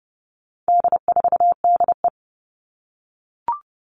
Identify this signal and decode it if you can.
Morse code, heard as a keyed tone, about 30 wpm, 700 Hz. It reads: D4DE
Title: All Cut Numbers - 4-Digit 30wpm